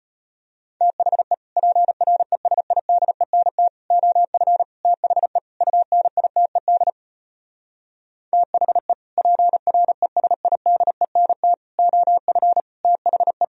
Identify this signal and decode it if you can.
Morse code, heard as a keyed tone, about 33 wpm, 700 Hz. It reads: THE PRESIDENT OF THE UNITED THE PRESIDENT OF THE